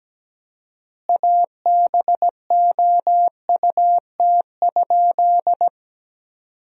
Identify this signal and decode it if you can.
Morse code, heard as a keyed tone, about 17 wpm, 700 Hz. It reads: ABOUT?
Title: 4x Word-Spacing - Sentences from Top 100 Words - 17wpm